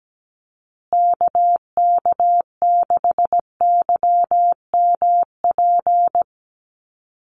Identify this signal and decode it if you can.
Morse code, heard as a keyed tone, about 17 wpm, 700 Hz. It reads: KK6YMP
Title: USA Call Signs Repeated 3x 17wpm